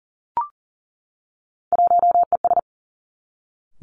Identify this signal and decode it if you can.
Morse code, heard as a keyed tone, about 41 wpm, 700 Hz. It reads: 1ES
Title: Three Challenging Letter-Number Combinations 40wpm